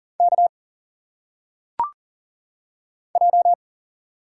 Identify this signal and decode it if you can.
Morse code, heard as a keyed tone, about 40 wpm, 700 Hz. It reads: K J
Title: Single Letters - Phonetic Pronunciation 40wpm